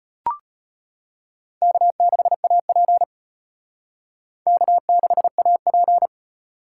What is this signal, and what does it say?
Morse code, heard as a keyed tone, about 33 wpm, 700 Hz. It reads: K6AP K6AP